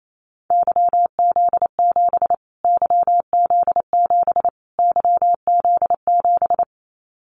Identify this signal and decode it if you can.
Morse code, heard as a keyed tone, about 28 wpm, 700 Hz. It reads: YZ7 YZ7 YZ7
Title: Callsign Prefixes for Top 25 DXCC - Repeated 3x - 28wpm